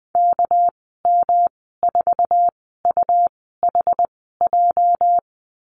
Morse code, heard as a keyed tone, about 20 wpm, spaced as for 14 wpm, 700 Hz. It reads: KM4UHJ
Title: 213 Lesson 34g - Number 8 - Call Signs 20wpm